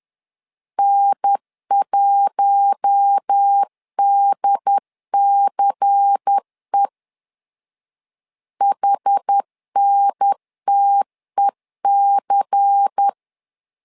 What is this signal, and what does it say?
Morse code, heard as a keyed tone, about 10 wpm, 800 Hz. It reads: N1DCE HNTEC
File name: LEZIONI / 5 LBVC / 5 LBVC 10wpm 2349